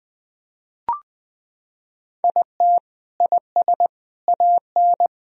Morse code, heard as a keyed tone, about 21 wpm, 700 Hz. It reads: IT IS AN
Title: Sentences from Top 1000 Words - Review Part 4 - 20wpm